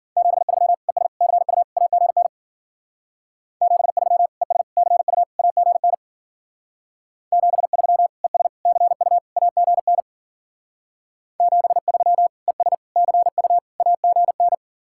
Morse code, heard as a keyed tone, about 52 wpm, 700 Hz. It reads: 73 ES CU AGN 73 ES CU AGN 73 ES CU AGN 73 ES CU AGN